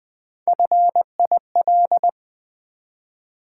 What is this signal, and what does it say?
Morse code, heard as a keyed tone, about 20 wpm, 700 Hz. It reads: FIL
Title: Build A Word - 1 Letter At A Time - 4-Letter Words - 20wpm